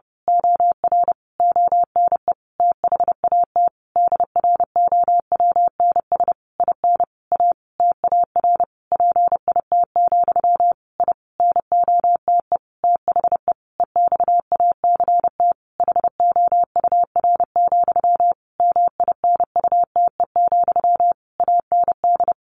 Morse code, heard as a keyed tone, about 30 wpm, 700 Hz. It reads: OR ONE THAT DROWNS IN A TAR PIT, I NOTE THE EXACT HOUR, MINUTE, AND
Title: A Sound Of Thunder 30wpm